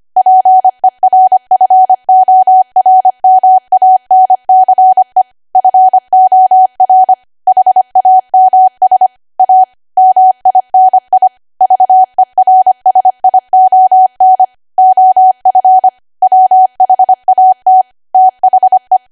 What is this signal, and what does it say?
Morse code, heard as a keyed tone, about 25 wpm, 750 Hz. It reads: PERFORMANCE FOR HAMS A MINI VERSION OF WHAT THE
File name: Archive / 25 WPM / 25WPM